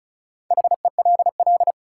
Morse code, heard as a keyed tone, about 36 wpm, 700 Hz. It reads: HELL
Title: Sets of 5 Words Encore - New Word Drill - 35wpm